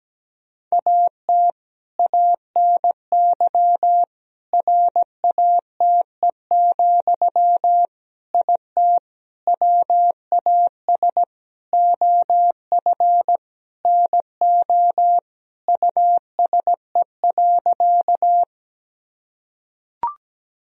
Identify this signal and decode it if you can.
Morse code, heard as a keyed tone, about 17 wpm, 700 Hz. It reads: AT ANY RATE, IT WAS OF NO USE.
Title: Sentences from Top 1000 Words - Review Part 4 - 17wpm